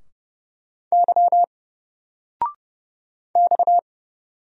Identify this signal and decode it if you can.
Morse code, heard as a keyed tone, about 30 wpm, 700 Hz. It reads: Y X
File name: Single-Letter-Number-30wpm